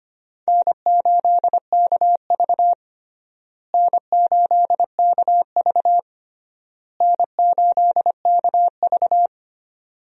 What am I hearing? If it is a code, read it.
Morse code, heard as a keyed tone, about 25 wpm, 700 Hz. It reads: N8KV N8KV N8KV
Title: USA Call Signs - 4 Characters - Repeated 3x - 25wpm